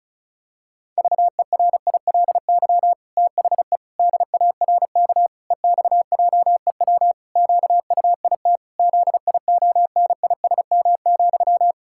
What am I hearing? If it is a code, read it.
Morse code, heard as a keyed tone, about 36 wpm, 700 Hz. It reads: VERILY THE DARK EXJEW QUIT ZIONISM,
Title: Pangram Challenge v2 - 35wpm